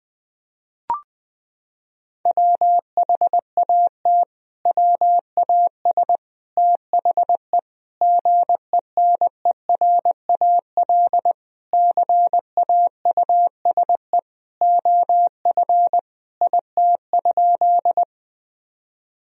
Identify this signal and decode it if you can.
Morse code, heard as a keyed tone, about 20 wpm, 700 Hz. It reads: WHAT WAS THE GENERAL CAUSE OF IT?
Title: Sentences from Top 1000 Words - Review Part 3 - 20wpm